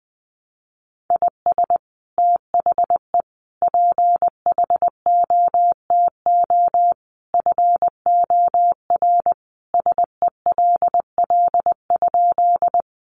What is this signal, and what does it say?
Morse code, heard as a keyed tone, about 20 wpm, 700 Hz. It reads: IS THE PHOTO FOR SELL?
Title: Sentences from Top 1200 Words 20wpm